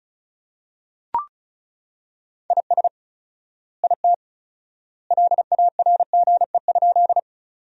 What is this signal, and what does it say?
Morse code, heard as a keyed tone, about 36 wpm, 700 Hz. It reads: IS IT LARGE?
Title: 4x Word-Spacing - Sentences from Top 200 Words - 35wpm